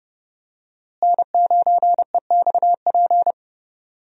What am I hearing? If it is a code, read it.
Morse code, heard as a keyed tone, about 30 wpm, 700 Hz. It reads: N9EXP
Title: USA Call Signs - 5 Characters - Repeated 3x - 30wpm